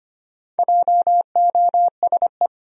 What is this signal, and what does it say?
Morse code, heard as a keyed tone, about 25 wpm, 700 Hz. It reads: JOSE
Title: USA Common Names - Max 5 Letters - 25wpm